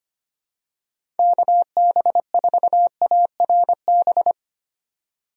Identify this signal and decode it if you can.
Morse code, heard as a keyed tone, about 25 wpm, 700 Hz. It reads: KB4ARB
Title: USA Call Signs - 6 Characters - 25wpm